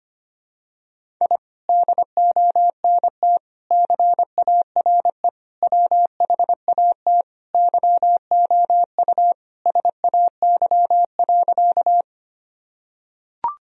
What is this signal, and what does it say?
Morse code, heard as a keyed tone, about 25 wpm, 700 Hz. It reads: I DONT CARE WHAT YOU SAY.
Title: Sentences from Top 1000 Words - Review Part 1 - 25wpm